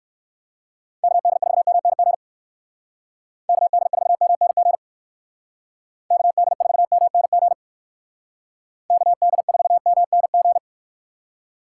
Morse code, heard as a keyed tone, about 58 wpm, 700 Hz. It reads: KD4MNG KD4MNG KD4MNG KD4MNG